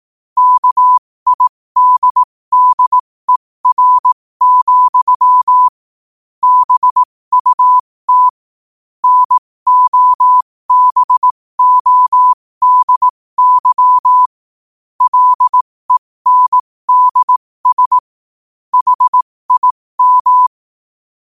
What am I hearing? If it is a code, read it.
Morse code, heard as a keyed tone, about 18 wpm, spaced as for 15 wpm, 1000 Hz. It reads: KIDDER, BUT NOBODY LENDS HIM
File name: qotdfiles / 15wpm